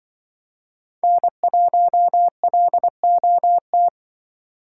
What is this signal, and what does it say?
Morse code, heard as a keyed tone, about 24 wpm, 700 Hz. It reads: N1LOT
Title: Speed Racing - USA 1x3 Call Signs 20wpm